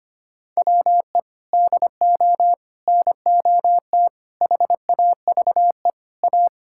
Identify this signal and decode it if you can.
Morse code, heard as a keyed tone, about 25 wpm, 700 Hz. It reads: WE DO NOT HAVE A
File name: practice / 25wpm / Sentences with 3-Word Phrases - 25wpm